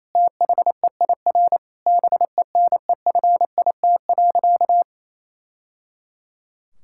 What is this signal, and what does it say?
Morse code, heard as a keyed tone, about 28 wpm, 700 Hz. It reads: THEIR BENEFIT.
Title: Sentences from Top 1000 Words - Review Part 6 - 28wpm